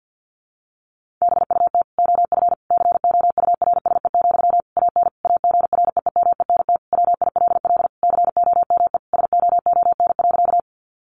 Morse code, heard as a keyed tone, about 50 wpm, 700 Hz. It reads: BUT OF COURSE, AN AGREEMENT WILL COME SOON.
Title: Sentences from Top 1500 Words 50wpm